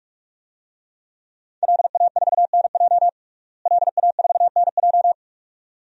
Morse code, heard as a keyed tone, about 46 wpm, 700 Hz. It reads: LA4NJ LA4NJ
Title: DXCC-100 Call Signs - 5 Characters - Repeated 2x - 45wpm